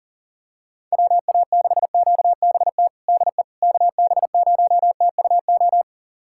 Morse code, heard as a keyed tone, about 40 wpm, 700 Hz. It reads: WA6QBT DE KB0TUO
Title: USA Callsign DE Callsign 40wpm